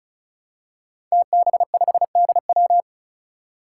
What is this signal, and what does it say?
Morse code, heard as a keyed tone, about 36 wpm, 700 Hz. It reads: TB5DW
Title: Five Challenging Letter-Number Combinations 35wpm